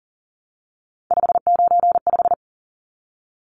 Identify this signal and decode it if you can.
Morse code, heard as a keyed tone, about 40 wpm, 700 Hz. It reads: 595
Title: Numbers - 3-Digit 40wpm